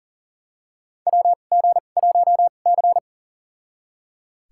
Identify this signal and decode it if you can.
Morse code, heard as a keyed tone, about 38 wpm, spaced as for 28 wpm, 690 Hz. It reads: WG1C
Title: USA Call Signs - 4 Characters - 2x Character Spacing - 40wpm